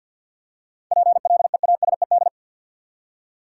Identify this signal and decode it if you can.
Morse code, heard as a keyed tone, about 52 wpm, 700 Hz. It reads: PLEASED
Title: Seven Letter Words - 50wpm